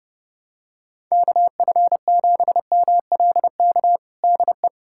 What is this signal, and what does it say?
Morse code, heard as a keyed tone, about 30 wpm, 700 Hz. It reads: KF7MLK DE
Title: USA Callsign DE Callsign 30wpm